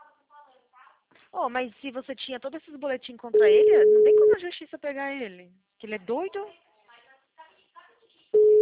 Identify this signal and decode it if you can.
Ringback tone (Europe, 425 Hz), with speech.
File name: out- -2020- - - .2